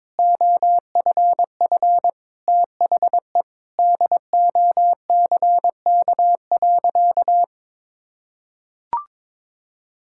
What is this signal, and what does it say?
Morse code, heard as a keyed tone, about 22 wpm, 700 Hz. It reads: OFF THE DOCK.
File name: practice / 22wpm / Sentences with 2-Word Phrases - 22wpm